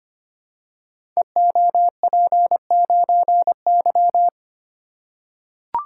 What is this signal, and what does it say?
Morse code, heard as a keyed tone, about 25 wpm, 700 Hz. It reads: EOP9Y
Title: Code Groups - 5 Letter:Numbers - 25wpm